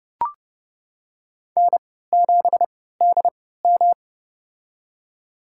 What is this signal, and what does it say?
Morse code, heard as a keyed tone, about 30 wpm, spaced as for 17 wpm, 700 Hz. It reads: N7DM
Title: USA Call Signs - 4 Characters - 3x Character Spacing - 30wpm